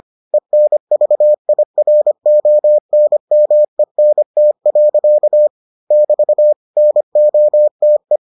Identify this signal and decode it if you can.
Morse code, heard as a keyed tone, about 25 wpm, 600 Hz. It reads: ENVIRONMENT. <BT> NOTE